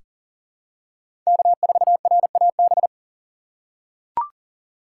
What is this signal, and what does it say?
Morse code, heard as a keyed tone, about 37 wpm, 700 Hz. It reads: K4RAB
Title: K1USN SST QSO Practice - 40wpm